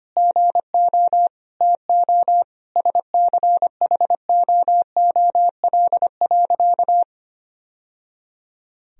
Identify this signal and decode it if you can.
Morse code, heard as a keyed tone, about 25 wpm, 700 Hz. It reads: GO TO SCHOOL.
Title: Sentences from Top 1000 Words - Review Part 1 - 25wpm